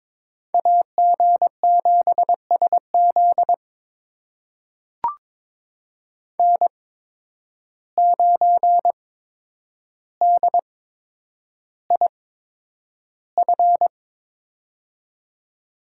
Morse code, heard as a keyed tone, about 22 wpm, spaced as for 9 wpm, 700 Hz. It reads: AG7SZ N9DIF
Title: USA Call Signs - 5 Characters - 8x Character Spacing - 22wpm